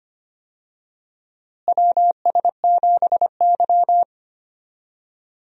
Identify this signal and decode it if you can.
Morse code, heard as a keyed tone, about 25 wpm, 700 Hz. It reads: WS7Y
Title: USA Call Signs - 4 Characters - Repeated 2x - 25wpm